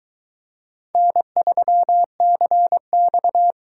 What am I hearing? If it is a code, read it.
Morse code, heard as a keyed tone, about 21 wpm, 700 Hz. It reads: N3CX